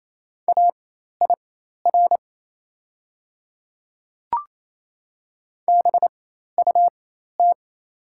Morse code, heard as a keyed tone, about 29 wpm, spaced as for 13 wpm, 690 Hz. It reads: AIR BUT
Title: 4x Character Spacing - Top 200 Words - 28wpm